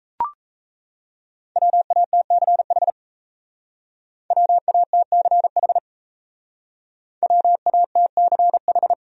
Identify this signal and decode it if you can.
Morse code, heard as a keyed tone, about 34 wpm, 700 Hz. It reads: WATCH WATCH WATCH
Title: Speed-Racing Top 500 Words 28wpm